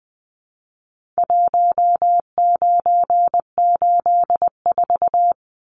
Morse code, heard as a keyed tone, about 20 wpm, 700 Hz. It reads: 1984
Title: Numbers - Contest Style - 4-Digit 20wpm